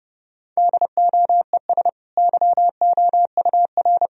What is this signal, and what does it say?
Morse code, heard as a keyed tone, about 30 wpm, 700 Hz. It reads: DOES YOUR
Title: Sentences from Top 1000 Words - Review Part 3 - 30wpm